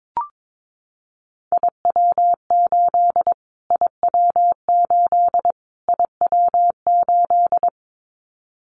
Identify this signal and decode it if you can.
Morse code, heard as a keyed tone, about 22 wpm, 700 Hz. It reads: IW8 IW8 IW8
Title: Callsign Prefixes for Top 25 DXCC - Repeated 3x - 22wpm